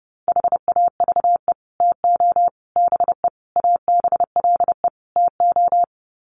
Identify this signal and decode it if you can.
Morse code, heard as a keyed tone, about 30 wpm, 700 Hz. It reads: HAVE TO BE ABLE TO
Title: Sets of 5 Words 30wpm